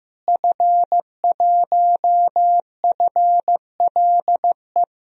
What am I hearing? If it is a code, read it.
Morse code, heard as a keyed tone, about 15 wpm, 700 Hz. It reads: F1FLE
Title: DXCC-100 Call Signs - 5 Characters - Repeated 2x - 15wpm